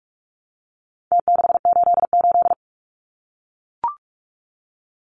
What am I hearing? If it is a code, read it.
Morse code, heard as a keyed tone, about 44 wpm, 700 Hz. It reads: T688
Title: Numbers - Contest Style - 4-Digit 45wpm